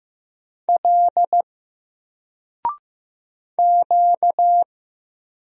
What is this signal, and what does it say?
Morse code, heard as a keyed tone, about 15 wpm, 700 Hz. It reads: L Q
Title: Single Character (Letter:Number:Punctuation:Prosign) 15wpm